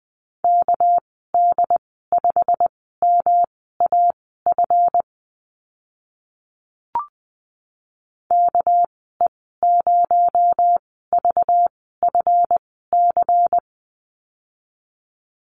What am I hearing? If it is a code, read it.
Morse code, heard as a keyed tone, about 20 wpm, spaced as for 14 wpm, 700 Hz. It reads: KD5MAF KE0VFC
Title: 219 Lesson 35g - Number 0 - Call Signs 20wpm